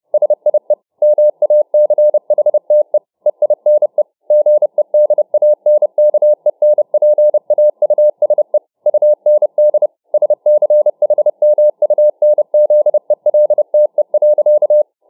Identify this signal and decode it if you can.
Morse code, heard as a keyed tone, about 31 wpm, 600 Hz. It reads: SIE MACHTE EINE GEDANKENPAUSE UND SCHMUNZELTE.